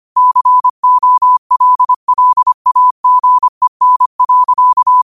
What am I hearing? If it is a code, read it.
Morse code, heard as a keyed tone, about 25 wpm, 1000 Hz. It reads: COLLAGEN.